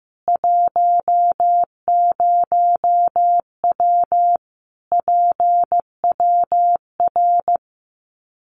Morse code, heard as a keyed tone, about 14 wpm, 700 Hz. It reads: 10W PWR
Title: Common QSO Phrases 15wpm